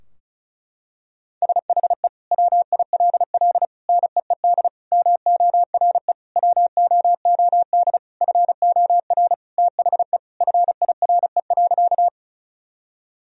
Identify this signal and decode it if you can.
Morse code, heard as a keyed tone, about 34 wpm, 700 Hz. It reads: SHE WILL NEED MORE WOOD FOR THE FIRE.
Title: Sentences from Top 1000 Words - Review Part 2 - 35wpm